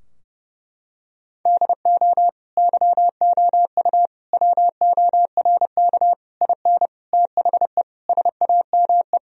Morse code, heard as a keyed tone, about 28 wpm, 700 Hz. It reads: DO YOU WORK IN THE SAME
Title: Sentences from Top 1000 Words - Review Part 4 - 30wpm